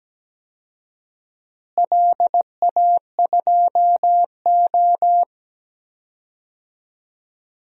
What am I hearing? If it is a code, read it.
Morse code, heard as a keyed tone, about 17 wpm, 700 Hz. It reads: LA2O
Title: DXCC-100 Call Signs - 4 Characters - Repeated 3x - 17wpm